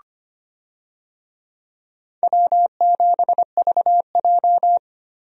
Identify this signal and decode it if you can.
Morse code, heard as a keyed tone, about 25 wpm, 700 Hz. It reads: W7VJ
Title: USA Call Signs - 4 Characters - Repeated 2x - 25wpm